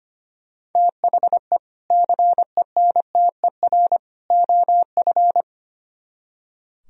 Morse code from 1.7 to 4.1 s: CENTER

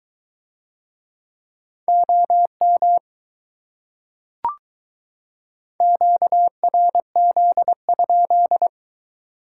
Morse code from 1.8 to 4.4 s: OM

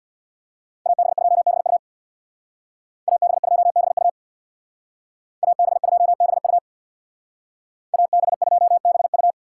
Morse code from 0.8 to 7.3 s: AB2BU AB2BU AB2BU

